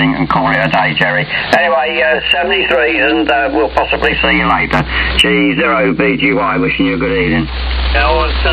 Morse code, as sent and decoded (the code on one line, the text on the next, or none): none